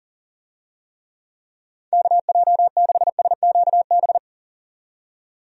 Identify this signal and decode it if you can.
Morse code, heard as a keyed tone, about 38 wpm, 700 Hz. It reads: KJ6SQB
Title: USA Call Signs - 6 Characters - 40wpm